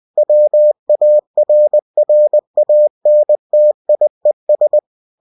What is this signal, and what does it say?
Morse code, heard as a keyed tone, about 20 wpm, 600 Hz. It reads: WARRANTIES
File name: cw-books / winnie-the-pooh / winnie-20wpm-page-0119